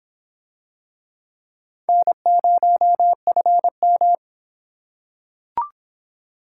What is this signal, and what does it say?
Morse code, heard as a keyed tone, about 25 wpm, 700 Hz. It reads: N0FM